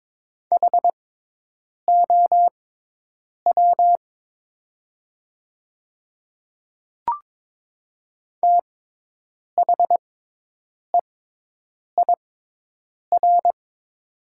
Morse code, heard as a keyed tone, about 22 wpm, spaced as for 8 wpm, 700 Hz. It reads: HOW THEIR
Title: 6x Character Spacing - Top 100 Words - 22wpm